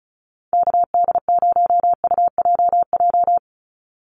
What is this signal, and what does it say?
Morse code, heard as a keyed tone, about 34 wpm, 700 Hz. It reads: KD0UJJ
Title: USA Call Signs 35wpm